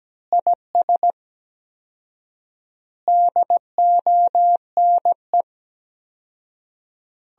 Morse code, heard as a keyed tone, about 17 wpm, 700 Hz. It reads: IS DONE
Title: 4x Word-Spacing - Sentences from Top 500 Words - 17wpm